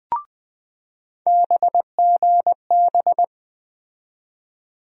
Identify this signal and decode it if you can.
Morse code, heard as a keyed tone, about 20 wpm, 700 Hz. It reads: BGB
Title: USA Call Sign Postfixes - 3 Letters 20wpm